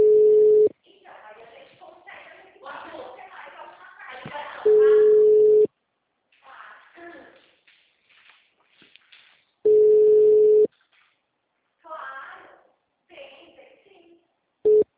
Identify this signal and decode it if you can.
Ringback tone (Europe, 425 Hz), with speech.
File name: out- -2007- - - .36187